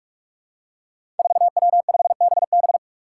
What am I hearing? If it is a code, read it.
Morse code, heard as a keyed tone, about 46 wpm, 700 Hz. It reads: 4W5BB
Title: Five Challenging Letter-Number Combinations 45wpm